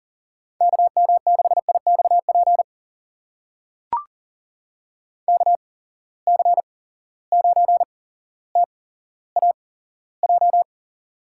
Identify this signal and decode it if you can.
Morse code, heard as a keyed tone, about 40 wpm, spaced as for 17 wpm, 700 Hz. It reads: KM6IXP KC9TAJ
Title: USA Call Signs - 6 Characters - 8x Character Spacing - 40wpm